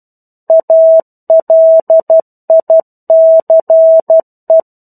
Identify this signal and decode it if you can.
Morse code, heard as a keyed tone, about 12 wpm, 650 Hz. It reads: ALICE
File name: 01-Chapter-01 12 WPM